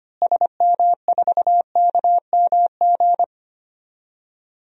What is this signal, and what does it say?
Morse code, heard as a keyed tone, about 25 wpm, 700 Hz. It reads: SM4KMG
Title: DXCC-100 Call Signs - 6 Characters - Repeated 3x - 25wpm